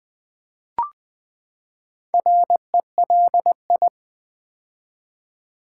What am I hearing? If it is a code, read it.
Morse code, heard as a keyed tone, about 20 wpm, 700 Hz. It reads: RELI
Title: Build A Word - 4 Letters At A Time - 11-Letter Words - 20wpm